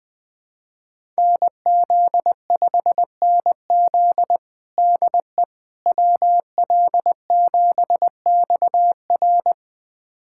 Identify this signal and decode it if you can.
Morse code, heard as a keyed tone, about 20 wpm, 700 Hz. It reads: NZ5NZ DE WL7XR
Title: USA Callsign DE Callsign - 5 Characters - 20wpm